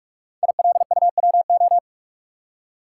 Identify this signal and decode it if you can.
Morse code, heard as a keyed tone, about 44 wpm, 700 Hz. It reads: ILUWO
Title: Code Groups - 5 Letters - 45wpm